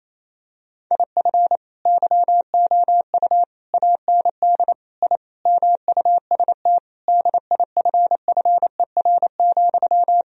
Morse code, heard as a keyed tone, about 28 wpm, 700 Hz. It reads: IF YOU AND I MUST DIFFER,
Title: Sentences from Top 1000 Words - Review Part 1 - 28wpm